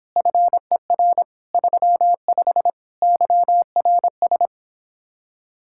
Morse code, heard as a keyed tone, about 29 wpm, 700 Hz. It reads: FER 35 YRS